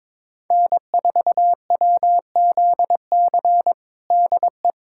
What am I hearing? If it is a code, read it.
Morse code, heard as a keyed tone, about 22 wpm, 700 Hz. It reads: N4WZC DE
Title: USA Callsign DE Callsign 22wpm